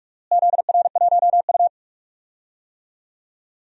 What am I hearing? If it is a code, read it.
Morse code, heard as a keyed tone, about 46 wpm, 700 Hz. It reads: ZR1U
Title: DXCC-100 Call Signs - 4 Characters - Repeated 2x - 45wpm